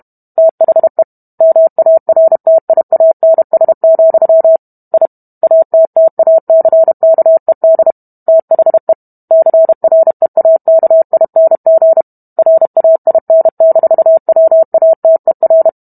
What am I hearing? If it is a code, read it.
Morse code, heard as a keyed tone, about 32 wpm, 650 Hz. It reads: THE MARTIANS, I ATTACKED THE CREAKING RAIN-WATER